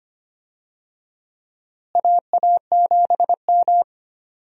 Morse code, heard as a keyed tone, about 25 wpm, 700 Hz. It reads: AA7M
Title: USA Call Signs - 4 Characters - Repeated 3x - 25wpm